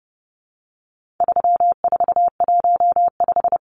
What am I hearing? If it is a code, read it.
Morse code, heard as a keyed tone, about 30 wpm, 700 Hz. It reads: 3415
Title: Numbers - 4-Digit 30wpm